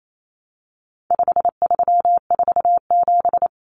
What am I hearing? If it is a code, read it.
Morse code, heard as a keyed tone, about 28 wpm, 700 Hz. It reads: 5347
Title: Common Cut Numbers - 4-Digit 28wpm